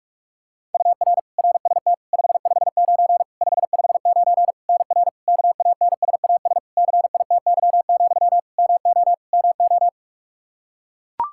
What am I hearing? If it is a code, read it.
Morse code, heard as a keyed tone, about 42 wpm, 700 Hz. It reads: UR RST 559 559 NR KANSAS CITY, MO MO
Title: K1USN SST QSO Practice 45wpm